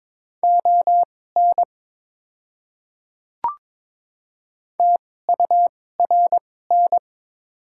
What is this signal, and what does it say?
Morse code, heard as a keyed tone, about 22 wpm, spaced as for 16 wpm, 690 Hz. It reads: ON TURN
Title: 2x Character Spacing - Top 300 Words - 22wpm